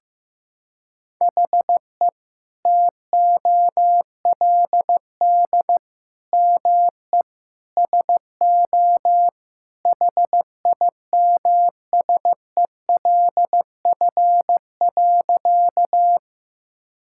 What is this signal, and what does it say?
Morse code, heard as a keyed tone, about 15 wpm, 700 Hz. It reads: HE TOLD ME SO HIMSELF.
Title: Sentences from Top 1600 Words 15wpm